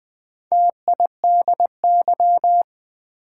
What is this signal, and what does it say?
Morse code, heard as a keyed tone, about 20 wpm, 700 Hz. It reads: TIDY
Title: Build A Word - 2 Letters At A Time - 4-Letter Words - 20wpm